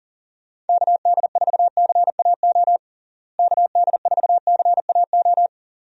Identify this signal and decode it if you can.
Morse code, heard as a keyed tone, about 37 wpm, 700 Hz. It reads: KD4CAO KD4CAO
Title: USA Call Signs - 6 Characters - Repeated 2x - 40wpm